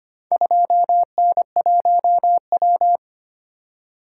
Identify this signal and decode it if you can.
Morse code, heard as a keyed tone, about 25 wpm, 700 Hz. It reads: 2N1W
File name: practice / 25wpm / Four Letter-Number Combinations - 25wpm